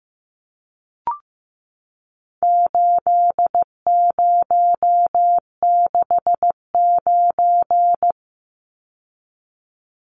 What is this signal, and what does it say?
Morse code, heard as a keyed tone, about 15 wpm, 700 Hz. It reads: 8069
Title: Numbers - 4-Digit 15wpm